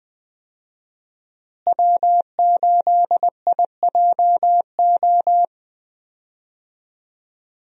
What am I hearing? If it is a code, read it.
Morse code, heard as a keyed tone, about 20 wpm, 700 Hz. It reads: W8IJO
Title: USA Call Signs - 5 Characters - Repeated 3x - 20wpm